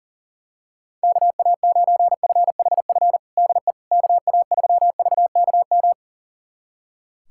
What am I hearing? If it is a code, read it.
Morse code, heard as a keyed tone, about 40 wpm, 700 Hz. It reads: KA9FHF DE KA3VKM
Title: USA Callsign DE Callsign - 6 Characters - 40wpm